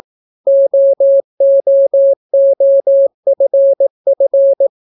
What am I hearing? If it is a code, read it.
Morse code, heard as a keyed tone, about 18 wpm, 550 Hz. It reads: OOOFF